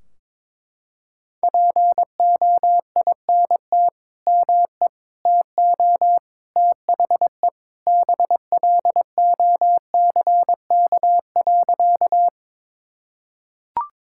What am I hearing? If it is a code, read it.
Morse code, heard as a keyed tone, about 21 wpm, 700 Hz. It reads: POINT ME TO THE BLOCK.
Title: Sentences from Top 1600 Words 22wpm